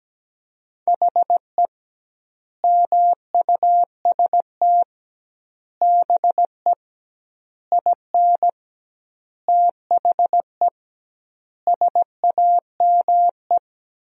Morse code, heard as a keyed tone, about 17 wpm, 700 Hz. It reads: HE MUST BE IN THE SAME
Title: 2x Word-Spacing - Sentences from Top 500 Words - 17wpm